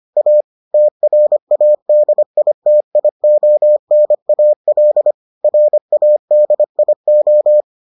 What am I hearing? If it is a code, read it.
Morse code, heard as a keyed tone, about 25 wpm, 600 Hz. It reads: A TRADITIONAL RADIO